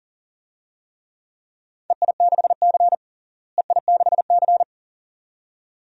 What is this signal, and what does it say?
Morse code, heard as a keyed tone, about 40 wpm, 700 Hz. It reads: EI6C EI6C
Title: DXCC-100 Call Signs - 4 Characters - Repeated 2x - 40wpm